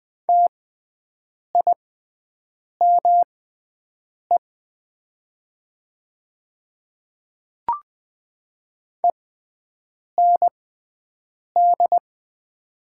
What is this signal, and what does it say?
Morse code, heard as a keyed tone, about 20 wpm, spaced as for 7 wpm, 700 Hz. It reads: TIME END
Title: 6x Character Spacing - Top 300 Words - 20wpm